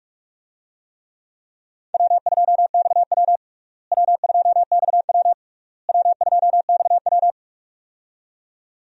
Morse code, heard as a keyed tone, about 46 wpm, 700 Hz. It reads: W2XW W2XW W2XW
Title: USA Call Signs - 4 Characters - Repeated 3x - 45wpm